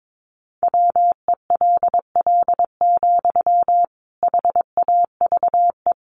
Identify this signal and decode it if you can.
Morse code, heard as a keyed tone, about 22 wpm, 700 Hz. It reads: WELL, HAVE
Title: Sentences from Top 800 Words 22wpm